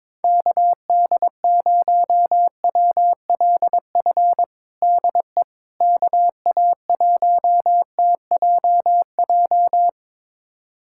Morse code, heard as a keyed tone, about 22 wpm, 700 Hz. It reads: KD0WLF DE KA1TJJ
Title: USA Callsign DE Callsign - 6 Characters - 22wpm